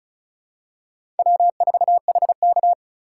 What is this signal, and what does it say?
Morse code, heard as a keyed tone, about 35 wpm, 700 Hz. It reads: W4HK
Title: USA Call Signs - 4 Characters - 2x Character Spacing - 35wpm